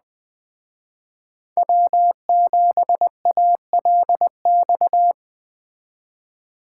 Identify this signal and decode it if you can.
Morse code, heard as a keyed tone, about 20 wpm, 700 Hz. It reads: W7ALX
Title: Speed Racing - USA Call Signs 17wpm